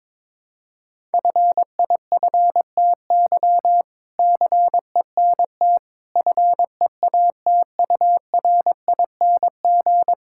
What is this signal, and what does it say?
Morse code, heard as a keyed tone, about 22 wpm, 700 Hz. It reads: FIFTY CENT FEATURING
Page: morse-code-ninja-practice-22wpm